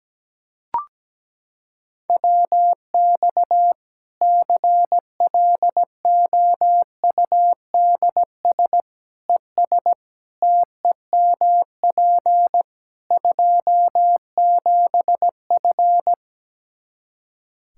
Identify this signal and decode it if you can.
Morse code, heard as a keyed tone, about 17 wpm, 700 Hz. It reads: WX CLOUDS ES TEMP 27F
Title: Practice QSO - Exchange 2 - Info - 17wpm